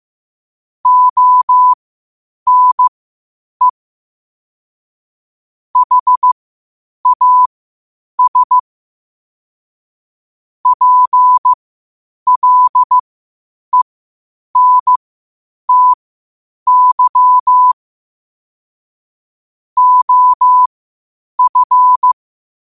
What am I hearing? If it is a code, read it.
Morse code, heard as a keyed tone, about 15 wpm, spaced as for 8 wpm, 1000 Hz. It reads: ONE HAS PLENTY OF
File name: qotdfiles / 8wpm